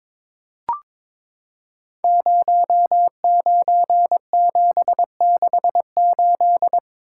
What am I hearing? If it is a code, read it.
Morse code, heard as a keyed tone, about 22 wpm, 700 Hz. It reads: 09768
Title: Code Groups - 5 Letter:Numbers - 22wpm